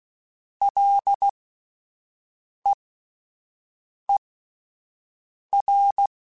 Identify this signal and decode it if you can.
Morse code, heard as a keyed tone, about 16 wpm, spaced as for 5 wpm, 800 Hz. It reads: LEER